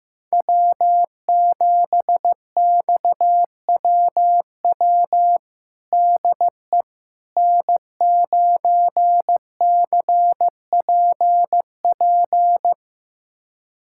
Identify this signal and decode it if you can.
Morse code, heard as a keyed tone, about 15 wpm, 700 Hz. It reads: W7XWW DE N9CPP
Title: USA Callsign DE Callsign - 5 Characters - 15wpm